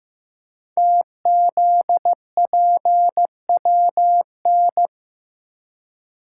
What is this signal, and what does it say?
Morse code, heard as a keyed tone, about 15 wpm, 700 Hz. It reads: TZPWN
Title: Code Groups - 5 Letter:Numbers - 15wpm